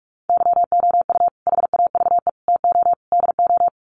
Morse code, heard as a keyed tone, about 44 wpm, 700 Hz. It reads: YOU HAVE TO DO
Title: Sets of 4 Words 45wpm